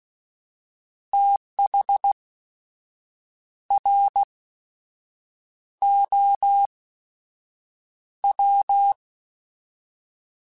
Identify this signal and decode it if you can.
Morse code, heard as a keyed tone, about 16 wpm, spaced as for 6 wpm, 800 Hz. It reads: THROW
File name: TH / TH mp3s / Fast TH @15